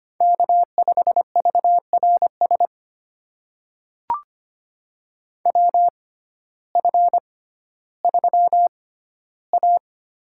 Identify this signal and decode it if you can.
Morse code, heard as a keyed tone, about 25 wpm, spaced as for 14 wpm, 700 Hz. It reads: K5VRS WF3A
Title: USA Call Signs - 6x Character Spacing - 25wpm